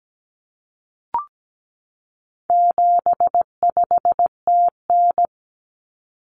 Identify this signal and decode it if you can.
Morse code, heard as a keyed tone, about 17 wpm, 700 Hz. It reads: 75TN
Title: Common Cut Numbers - 4-Digit 17wpm